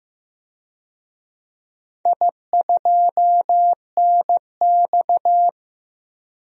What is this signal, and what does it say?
Morse code, heard as a keyed tone, about 15 wpm, 700 Hz. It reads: I2NX
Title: DXCC-100 Call Signs - 4 Characters - 15wpm